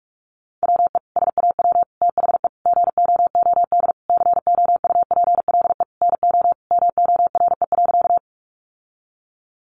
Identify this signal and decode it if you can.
Morse code, heard as a keyed tone, about 48 wpm, 700 Hz. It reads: WE SAW THE GOOD COUPLE NO MORE.
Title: Sentences from Top 1100 Words 45wpm